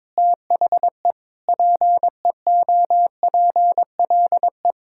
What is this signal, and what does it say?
Morse code, heard as a keyed tone, about 21 wpm, 700 Hz. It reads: THE PEOPLE
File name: Sentences with 9-Word Phrases - 22wpm